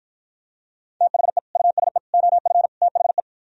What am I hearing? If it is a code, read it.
Morse code, heard as a keyed tone, about 52 wpm, 700 Hz. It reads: THE USE OF THE